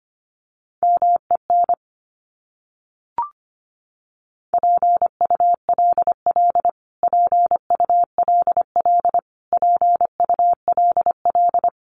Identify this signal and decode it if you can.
Morse code, heard as a keyed tone, about 25 wpm, 700 Hz. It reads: MEN PULL PULL PULL
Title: Top 500 Words Repeated 25wpm